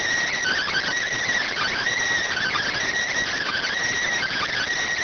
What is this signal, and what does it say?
Emergency Alert System (SAME) tones (header bursts), heard as harsh buzzing bursts. It is heard from 0.3 to 4.7 s.